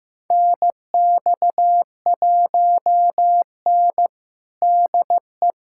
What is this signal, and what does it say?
Morse code, heard as a keyed tone, about 15 wpm, 700 Hz. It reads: NX1N DE
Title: USA Callsign DE Callsign - 4 Characters - 15wpm